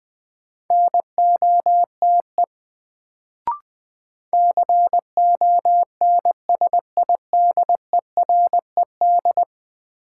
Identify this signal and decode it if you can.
Morse code, heard as a keyed tone, about 21 wpm, 700 Hz. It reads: NOTE CONSIDERED
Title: Sets of 5 Words - New Word Drill - 20wpm